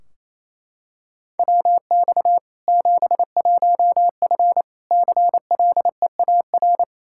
Morse code, heard as a keyed tone, about 28 wpm, 700 Hz. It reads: WX 71F CLEAR
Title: K1USN SST QSO Practice 28wpm